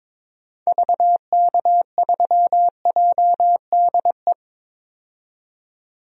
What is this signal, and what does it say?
Morse code, heard as a keyed tone, about 22 wpm, 700 Hz. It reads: VK3JDE
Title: DXCC-100 Call Signs - 6 Characters - 22wpm